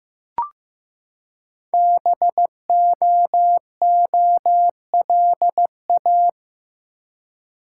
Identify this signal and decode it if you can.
Morse code, heard as a keyed tone, about 15 wpm, 700 Hz. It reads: BOOLA
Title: Code Groups - 5 Letters - 15wpm